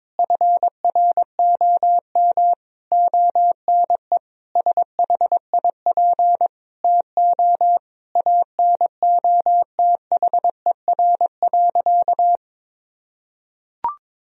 Morse code, heard as a keyed tone, about 22 wpm, 700 Hz. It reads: FROM ONE SHIP TO ANOTHER.
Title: Sentences from Top 1000 Words - Review Part 2 - 22wpm